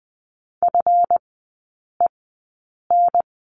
Morse code, heard as a keyed tone, about 20 wpm, spaced as for 8 wpm, 700 Hz. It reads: FEN
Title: 138 Lesson 23e - Letter f - Three Characters 20wpm